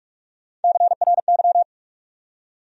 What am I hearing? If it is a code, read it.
Morse code, heard as a keyed tone, about 60 wpm, 700 Hz. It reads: CRY